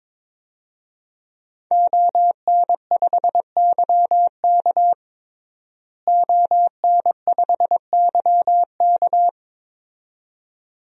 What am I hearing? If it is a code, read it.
Morse code, heard as a keyed tone, about 22 wpm, 700 Hz. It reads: ON5YK ON5YK
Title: DXCC-100 Call Signs - 5 Characters - Repeated 2x - 22wpm